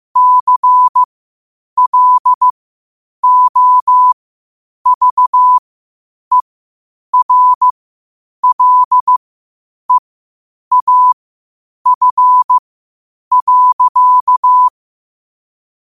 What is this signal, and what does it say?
Morse code, heard as a keyed tone, about 15 wpm, spaced as for 8 wpm, 1000 Hz. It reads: CLOVERLEAF.
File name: qotdfiles / 8wpm